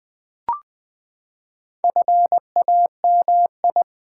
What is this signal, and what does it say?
Morse code, heard as a keyed tone, about 20 wpm, 700 Hz. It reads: FAMI
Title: Build A Word - 4 Letters At A Time - 8-Letter Words - 20wpm